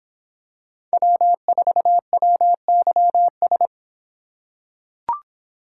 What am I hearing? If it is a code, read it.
Morse code, heard as a keyed tone, about 26 wpm, 700 Hz. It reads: W4WYS